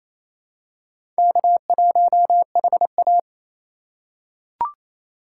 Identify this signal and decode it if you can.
Morse code, heard as a keyed tone, about 28 wpm, 700 Hz. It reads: K1HA
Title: USA Call Signs - 4 Characters - 2x Character Spacing - 28wpm